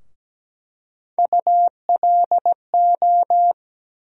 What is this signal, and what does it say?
Morse code, heard as a keyed tone, about 17 wpm, 700 Hz. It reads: ULO
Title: USA Call Sign Postfixes - 3 Letters 17wpm